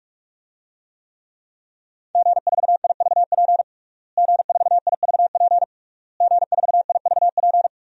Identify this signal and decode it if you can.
Morse code, heard as a keyed tone, about 42 wpm, 700 Hz. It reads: G4IVP G4IVP G4IVP
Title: DXCC-100 Call Signs - 5 Characters - Repeated 3x - 45wpm